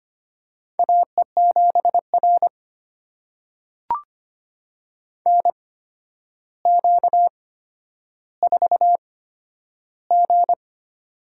Morse code, heard as a keyed tone, about 25 wpm, spaced as for 11 wpm, 700 Hz. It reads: AE7R NQ4G
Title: USA Call Signs - 4 Characters - 8x Character Spacing - 25wpm